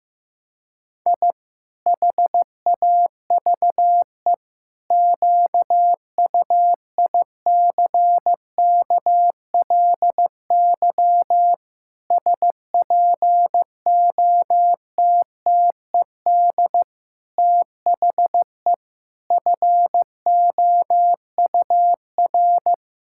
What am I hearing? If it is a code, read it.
Morse code, heard as a keyed tone, about 16 wpm, 700 Hz. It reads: I HAVE QUICKLY SPOTTED THE FOUR
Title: Pangram Challenge v2 - 15wpm